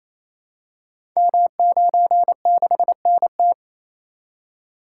Morse code, heard as a keyed tone, about 29 wpm, 690 Hz. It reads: M96NT
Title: Code Groups - 5 Letter:Numbers - 28wpm